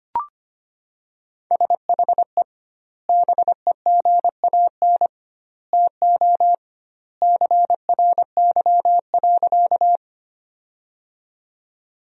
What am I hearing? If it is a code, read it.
Morse code, heard as a keyed tone, about 25 wpm, 700 Hz. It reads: SHE BEGAN TO CRY.
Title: 2x Word-Spacing - Sentences from Top 500 Words - 25wpm